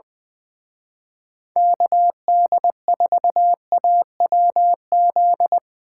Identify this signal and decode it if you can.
Morse code, heard as a keyed tone, about 22 wpm, 700 Hz. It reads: KD4AWZ
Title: Speed Racing - USA 2x3 Call Signs 20wpm